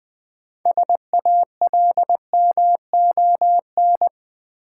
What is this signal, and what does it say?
Morse code, heard as a keyed tone, about 20 wpm, 700 Hz. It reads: SALMON
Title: Easy Words with Hints - 20wpm